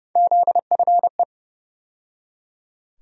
Morse code, heard as a keyed tone, about 30 wpm, 700 Hz. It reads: ZFE
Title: USA Call Sign Postfixes - 3 Letters 30wpm